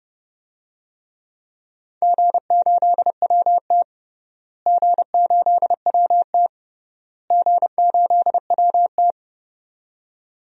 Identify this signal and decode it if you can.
Morse code, heard as a keyed tone, about 30 wpm, 700 Hz. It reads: G8WT G8WT G8WT
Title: DXCC-100 Call Signs - 4 Characters - Repeated 3x - 30wpm